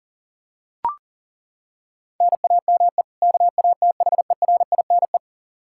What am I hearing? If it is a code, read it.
Morse code, heard as a keyed tone, about 40 wpm, 700 Hz. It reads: NAME KATHERINE
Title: Practice QSO - Exchange 1 - Beginning - 40wpm